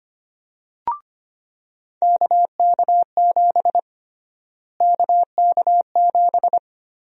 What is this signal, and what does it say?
Morse code, heard as a keyed tone, about 25 wpm, 700 Hz. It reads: KK7 KK7
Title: USA Call Sign Prefixes - Repeated 2x - 25wpm